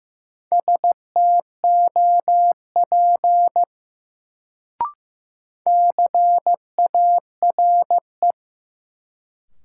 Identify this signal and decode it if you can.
Morse code, heard as a keyed tone, about 15 wpm, 700 Hz. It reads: STOP CARE
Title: Sentences from Top 300 Words - New Word Drill - 15wpm